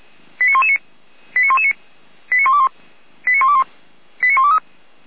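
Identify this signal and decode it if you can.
Five-tone selective call (ZVEI), heard as quick runs of five tones, tones of about 70 ms. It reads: 89119 89119 89121 89121 89123
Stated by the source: ZVEI five-tone call received by SDR, from the Feuerwehr Orlamünde fire brigade's audio archive.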